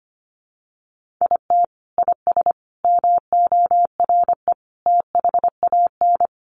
Morse code, heard as a keyed tone, about 25 wpm, 700 Hz. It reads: IT IS MORE THAN
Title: Sentences from Top 1400 Words 25wpm